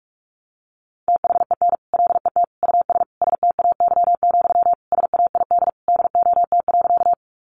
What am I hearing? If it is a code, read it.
Morse code, heard as a keyed tone, about 44 wpm, 700 Hz. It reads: THEN LET US STAY, SAID DOT.
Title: Sentences from Top 1500 Words 45wpm